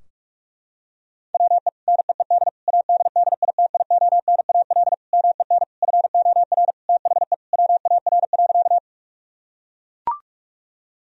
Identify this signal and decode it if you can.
Morse code, heard as a keyed tone, about 41 wpm, 700 Hz. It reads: WE NEED ADDITIONAL MEN FOR THE WAR.
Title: Sentences from Top 1000 Words - Review Part 4 - 45wpm